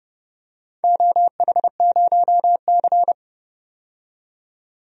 Morse code, heard as a keyed tone, about 31 wpm, 700 Hz. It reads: OH0C
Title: DXCC-100 Call Signs - 4 Characters - 30wpm